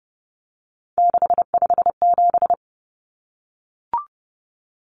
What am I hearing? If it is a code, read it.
Morse code, heard as a keyed tone, about 30 wpm, 700 Hz. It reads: 657
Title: Common Cut Numbers - 3-Digit 30wpm